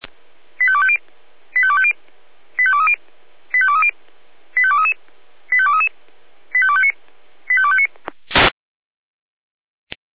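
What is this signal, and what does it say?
Five-tone selective call (ZVEI), heard as quick runs of five tones, tones of about 70 ms. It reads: 85250 85250 85230 85230 85233 85233 85259 85259
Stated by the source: ZVEI five-tone call 85250 received by SDR.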